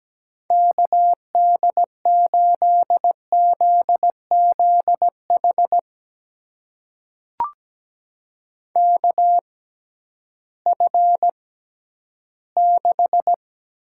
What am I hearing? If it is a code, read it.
Morse code, heard as a keyed tone, about 17 wpm, spaced as for 11 wpm, 700 Hz. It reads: KD8ZZH KF6
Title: USA Call Signs - 6 Characters - 6x Character Spacing - 17wpm